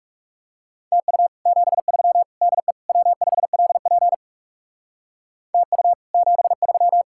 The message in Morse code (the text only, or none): TU 73 DE W5LP TU 73